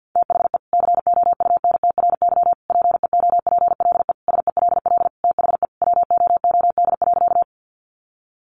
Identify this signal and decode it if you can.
Morse code, heard as a keyed tone, about 49 wpm, 700 Hz. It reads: THE COUNTRY PEOPLE SELL THE WOOD.
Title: Sentences from Top 1100 Words 50wpm